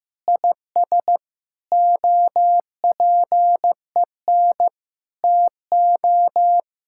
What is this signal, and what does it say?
Morse code, heard as a keyed tone, about 15 wpm, 700 Hz. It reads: IS OPEN TO
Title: Sentences from Top 1000 Words - Review Part 6 - 15wpm